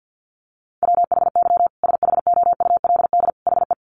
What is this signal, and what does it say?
Morse code, heard as a keyed tone, about 50 wpm, 700 Hz. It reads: WHY SHOULD HE
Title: Sentences from Top 1300 Words 50wpm